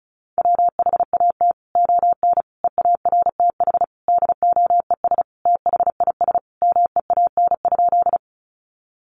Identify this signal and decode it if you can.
Morse code, heard as a keyed tone, about 34 wpm, 700 Hz. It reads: WHAT ON EARTH DOES THIS MEAN?
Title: Sentences from Top 200 Words 35wpm